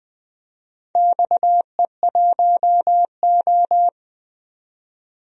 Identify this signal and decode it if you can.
Morse code, heard as a keyed tone, about 20 wpm, 700 Hz. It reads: XE1O
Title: DXCC-100 Call Signs - 4 Characters - 20wpm